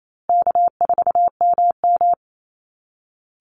Morse code, heard as a keyed tone, about 28 wpm, 700 Hz. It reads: K4MM
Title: Easy Call Signs Repeated 3x - 28wpm